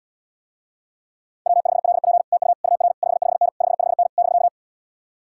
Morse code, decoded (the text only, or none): W5LP TU UR 55N 55N <BK>